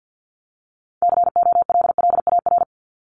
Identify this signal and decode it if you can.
Morse code, heard as a keyed tone, about 50 wpm, 700 Hz. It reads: COLLAR
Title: Top 1751-2000 Words 50wpm